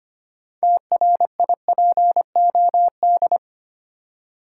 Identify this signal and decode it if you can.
Morse code, heard as a keyed tone, about 25 wpm, 690 Hz. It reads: TRIPOD
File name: practice / 25wpm / Easy Words with Hints - 25wpm